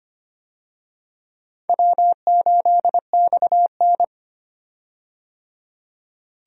Morse code, heard as a keyed tone, about 25 wpm, 700 Hz. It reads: W8XN
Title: USA Call Signs - 4 Characters - 25wpm